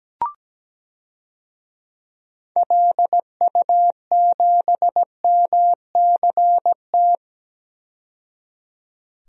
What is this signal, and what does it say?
Morse code, heard as a keyed tone, about 17 wpm, 700 Hz. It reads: LU7MCT
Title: DXCC-100 Call Signs - 6 Characters - 17wpm